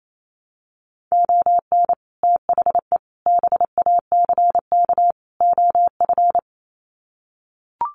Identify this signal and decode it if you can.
Morse code, heard as a keyed tone, about 28 wpm, 700 Hz. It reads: ON THE BACK OF
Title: Sets of 4 Words 28wpm